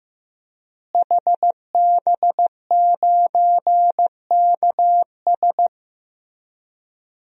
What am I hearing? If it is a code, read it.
Morse code, heard as a keyed tone, about 15 wpm, 700 Hz. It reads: HB9KS
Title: DXCC-100 Call Signs - 5 Characters - Repeated 2x - 15wpm